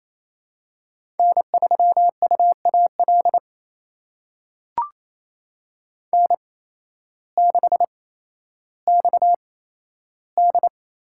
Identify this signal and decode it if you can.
Morse code, heard as a keyed tone, about 28 wpm, spaced as for 13 wpm, 700 Hz. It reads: N3UAL N6XD
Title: USA Call Signs - 5 Characters - 8x Character Spacing - 28wpm